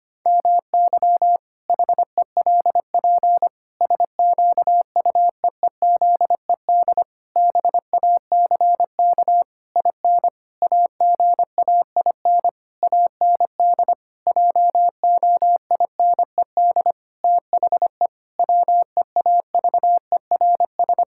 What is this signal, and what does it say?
Morse code, heard as a keyed tone, about 25 wpm, 700 Hz. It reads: MY HELP SQUEEZED BACK IN AGAIN AND JOINED THE WEAVERS
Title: Pangram Challenge v2 - 25wpm